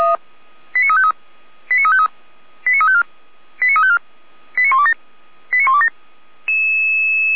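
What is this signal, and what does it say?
Five-tone selective call (ZVEI), heard as quick runs of five tones, tones of about 70 ms. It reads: 89353 89353 89354 89354 89127 89127
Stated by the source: ZVEI five-tone selective call received on SDR, fire brigade feuerwehr-orlamuende.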